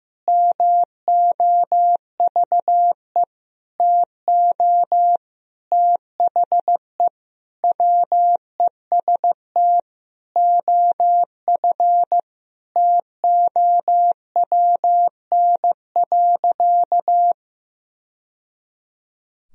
Morse code, heard as a keyed tone, about 15 wpm, 700 Hz. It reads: MOVE TO THE WEST OF TOWN.
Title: Sentences from Top 1000 Words - Review Part 2 - 15wpm